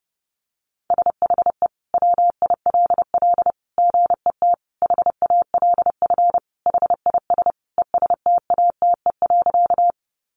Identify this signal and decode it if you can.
Morse code, heard as a keyed tone, about 30 wpm, 700 Hz. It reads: SHE WILL GET HALF HIS ESTATE.
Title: Sentences from Top 1200 Words 30wpm